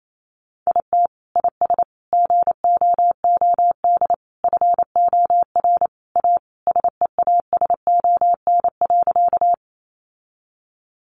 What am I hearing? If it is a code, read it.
Morse code, heard as a keyed tone, about 28 wpm, 700 Hz. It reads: IT IS GOOD FOR A SEASON.
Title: Sentences from Top 1400 Words 28wpm